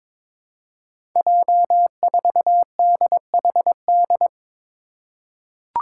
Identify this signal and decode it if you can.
Morse code, heard as a keyed tone, about 21 wpm, 700 Hz. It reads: J4DHD
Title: Five Challenging Letter-Number Combinations 22wpm